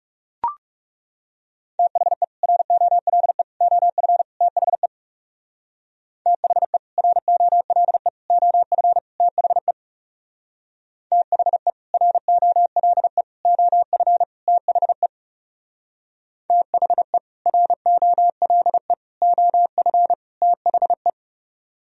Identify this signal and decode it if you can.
Morse code, heard as a keyed tone, about 37 wpm, 700 Hz. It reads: THE ROLE OF THE THE ROLE OF THE THE ROLE OF THE THE ROLE OF THE
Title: Speed-Racing - Sets of 4 Words 30wpm